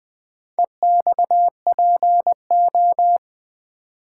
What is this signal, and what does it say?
Morse code, heard as a keyed tone, about 20 wpm, 700 Hz. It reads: EXPO
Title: Build A Word - 2 Letters At A Time - 8-Letter Words - 20wpm